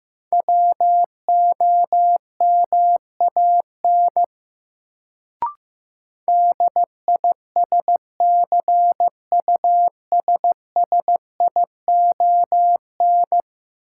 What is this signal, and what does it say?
Morse code, heard as a keyed tone, about 15 wpm, 700 Hz. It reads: WOMAN DISCUSSION
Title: Sentences from Top 1100 Words - New Word Drill Alt - 15wpm